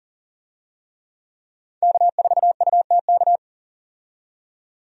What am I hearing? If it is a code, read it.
Morse code, heard as a keyed tone, about 40 wpm, 700 Hz. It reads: K4UTK
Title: USA Call Signs - 5 Characters - 40wpm